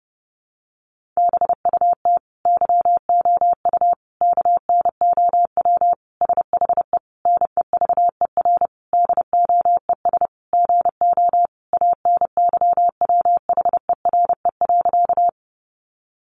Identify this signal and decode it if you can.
Morse code, heard as a keyed tone, about 28 wpm, 700 Hz. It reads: BUT YOU KNOW SHE NEVER DOES GO ANYWHERE.
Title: Sentences from Top 1400 Words 30wpm